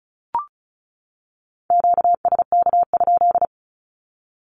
Morse code, heard as a keyed tone, about 34 wpm, 700 Hz. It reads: QSK?
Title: Q-Codes 35wpm